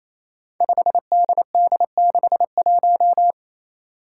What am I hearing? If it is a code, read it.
Morse code, heard as a keyed tone, about 28 wpm, 700 Hz. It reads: 5DD61
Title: Five Challenging Letter-Number Combinations 28wpm